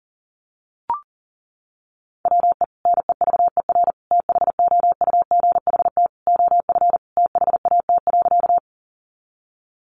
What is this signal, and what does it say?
Morse code, heard as a keyed tone, about 40 wpm, 700 Hz. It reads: WE NEVER THOUGHT OF THAT.
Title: Sentences from Top 300 Words 40wpm